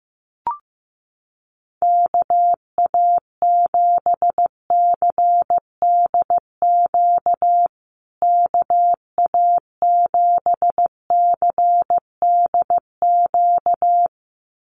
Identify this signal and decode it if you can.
Morse code, heard as a keyed tone, about 15 wpm, 700 Hz. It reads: KA7CDQ KA7CDQ
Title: USA Call Signs Repeated 2x 15wpm